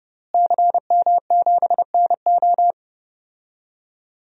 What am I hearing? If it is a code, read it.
Morse code, heard as a keyed tone, about 30 wpm, 700 Hz. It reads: CM7NO
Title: DXCC-100 Call Signs - 5 Characters - Repeated 3x - 30wpm